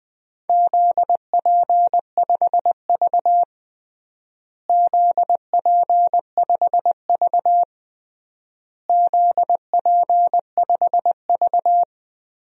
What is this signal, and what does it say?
Morse code, heard as a keyed tone, about 20 wpm, 700 Hz. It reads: ZP5V ZP5V ZP5V
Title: DXCC-100 Call Signs - 4 Characters - Repeated 3x - 20wpm